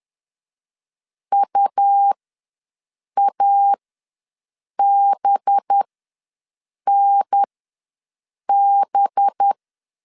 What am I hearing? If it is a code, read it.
Morse code, heard as a keyed tone, about 10 wpm, spaced as for 6 wpm, 800 Hz. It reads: UABNB
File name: LEZIONI / 5 LBVC / 5 LB ws 10wpm 3616